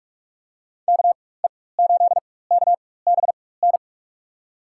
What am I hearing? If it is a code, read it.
Morse code, heard as a keyed tone, about 46 wpm, spaced as for 21 wpm, 700 Hz. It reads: KE8KBN
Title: USA Call Signs - 4x Character Spacing - 45wpm